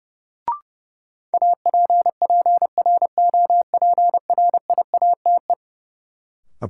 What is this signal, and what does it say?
Morse code, heard as a keyed tone, about 30 wpm, 700 Hz. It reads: APPROPRIATE
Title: Sentences from Top 900 Words - New Word Drill - 30wpm